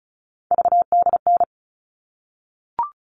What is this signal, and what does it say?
Morse code, heard as a keyed tone, about 34 wpm, 700 Hz. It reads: VDN
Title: All Cut Numbers - 3-Digit 35wpm